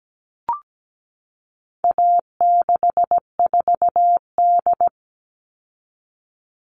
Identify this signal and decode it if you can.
Morse code, heard as a keyed tone, about 17 wpm, 700 Hz. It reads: A64D
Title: All Cut Numbers - 4-Digit 17wpm